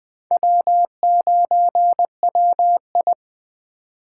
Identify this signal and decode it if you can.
Morse code, heard as a keyed tone, about 21 wpm, 700 Hz. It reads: W9WI